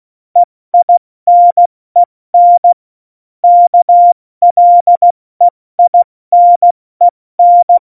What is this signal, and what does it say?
Morse code, heard as a keyed tone, about 16 wpm, 700 Hz. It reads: EINEN KLEINEN
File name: Morsekurs / Witze / Witz 14. August 2018, 70 F80